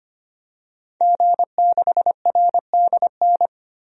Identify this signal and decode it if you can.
Morse code, heard as a keyed tone, about 25 wpm, 700 Hz. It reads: G6RDN
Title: DXCC-100 Call Signs - 5 Characters - Repeated 2x - 25wpm